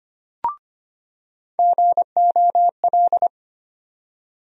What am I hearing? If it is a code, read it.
Morse code, heard as a keyed tone, about 25 wpm, 700 Hz. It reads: GOL
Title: USA Call Sign Postfixes - 3 Letters 25wpm